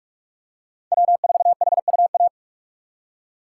Morse code, heard as a keyed tone, about 46 wpm, 700 Hz. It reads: W4HUA
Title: USA Call Signs - 5 Characters - 2x Character Spacing - 45wpm